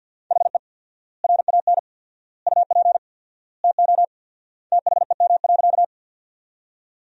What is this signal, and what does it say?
Morse code, heard as a keyed tone, about 47 wpm, 700 Hz. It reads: HE RAN UP TO THEM.
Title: 4x Word-Spacing - Sentences from Top 500 Words - 50wpm